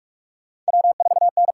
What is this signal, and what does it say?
Morse code, heard as a keyed tone, about 60 wpm, spaced as for 53 wpm, 700 Hz. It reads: W4N
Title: Speed Racing - USA Call Signs 45wpm